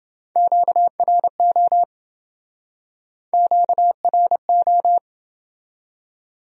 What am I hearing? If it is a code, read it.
Morse code, heard as a keyed tone, about 27 wpm, 700 Hz. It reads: QRO QRO